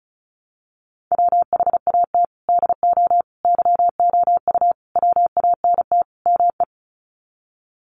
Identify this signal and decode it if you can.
Morse code, heard as a keyed tone, about 34 wpm, 700 Hz. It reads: WHAT DO YOU WANT ME
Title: Sets of 5 Words Encore2 35wpm